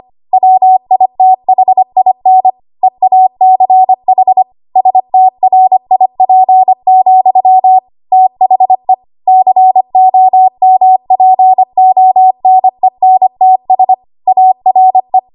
Morse code, heard as a keyed tone, about 25 wpm, 750 Hz. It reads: WITHIN EACH STRIP, THE COMPONENTS ARE
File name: Morse / 25WPM